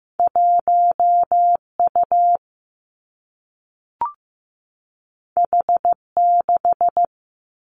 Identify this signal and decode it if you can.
Morse code, heard as a keyed tone, about 15 wpm, 700 Hz. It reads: 1 U H 6
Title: Two Challenging Letter-Number Combinations 15wpm